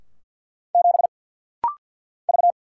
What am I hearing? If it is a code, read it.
Morse code, heard as a keyed tone, about 50 wpm, 690 Hz. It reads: 7 V
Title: Single Letter-Number Alt - Rapid-Fire 50wpm